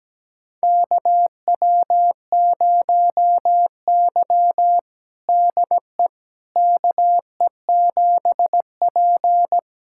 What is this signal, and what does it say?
Morse code, heard as a keyed tone, about 17 wpm, 700 Hz. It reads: KW0Y DE KE7P
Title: USA Callsign DE Callsign - 4 Characters - 17wpm